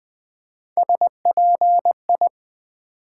Morse code, heard as a keyed tone, about 20 wpm, 700 Hz. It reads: SPI
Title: Build A Word - 1 Letter At A Time - 5-Letter Words - 20wpm